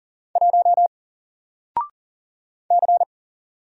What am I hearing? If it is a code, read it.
Morse code, heard as a keyed tone, about 39 wpm, 700 Hz. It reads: 1 C
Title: Single Character (Letter:Number:Punctuation:Prosign) 40wpm